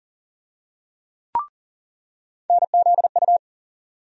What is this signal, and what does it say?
Morse code, heard as a keyed tone, about 40 wpm, 700 Hz. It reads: NZU
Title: USA Call Sign Postfixes - 3 Letters 40wpm